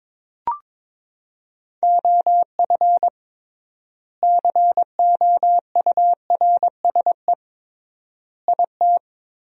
Morse code, heard as a keyed tone, about 22 wpm, 700 Hz. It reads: OF COURSE IT
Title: 3x Word-Spacing - Sentences from Top 500 Words - 22wpm